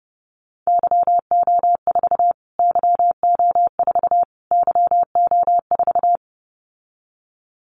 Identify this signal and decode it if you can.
Morse code, heard as a keyed tone, about 30 wpm, 700 Hz. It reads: YO4 YO4 YO4
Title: Callsign Prefixes for Top 25 DXCC - Repeated 3x - 30wpm